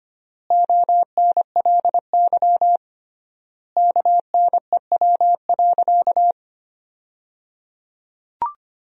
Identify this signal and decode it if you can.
Morse code, heard as a keyed tone, about 25 wpm, 700 Hz. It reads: ONLY KNEW.
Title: 3x Word-Spacing - Sentences from Top 500 Words - 25wpm